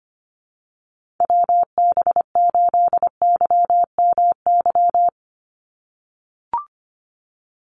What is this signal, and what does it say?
Morse code, heard as a keyed tone, about 24 wpm, 700 Hz. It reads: WB8YMY
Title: USA Call Signs 25wpm